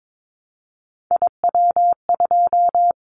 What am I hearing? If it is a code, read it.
Morse code, heard as a keyed tone, about 22 wpm, 700 Hz. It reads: IW2
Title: Callsign Prefixes for Top 25 DXCC - 22wpm